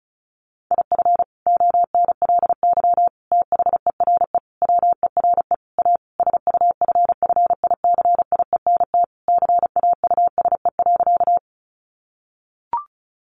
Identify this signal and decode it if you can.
Morse code, heard as a keyed tone, about 34 wpm, 700 Hz. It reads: IF ONLY THERE WERE A SUFFICIENT CAUSE.
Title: Sentences from Top 1400 Words 35wpm